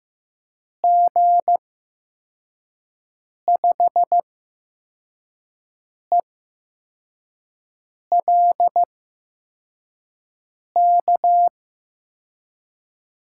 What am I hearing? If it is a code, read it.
Morse code, heard as a keyed tone, about 15 wpm, spaced as for 4 wpm, 700 Hz. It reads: G5ELK
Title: USA Call Signs - 6 Characters - 8x Character Spacing - 15wpm